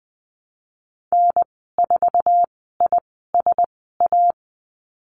Morse code, heard as a keyed tone, about 20 wpm, spaced as for 15 wpm, 700 Hz. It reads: N4ISA
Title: 088 Lesson 15g - Number 5 - Call Signs 20wpm